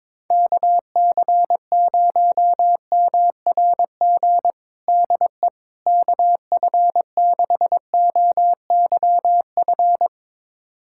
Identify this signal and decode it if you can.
Morse code, heard as a keyed tone, about 22 wpm, 700 Hz. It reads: KC0MRG DE KF6OYF
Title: USA Callsign DE Callsign - 6 Characters - 22wpm